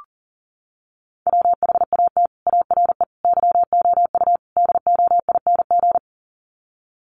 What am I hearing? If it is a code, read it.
Morse code, heard as a keyed tone, about 37 wpm, 700 Hz. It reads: WHAT ARE YOU DOING
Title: Sets of 4 Words 40wpm